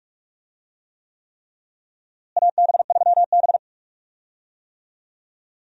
Morse code, heard as a keyed tone, about 43 wpm, 700 Hz. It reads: AB3B
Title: USA Call Signs - 4 Characters - Repeated 3x - 45wpm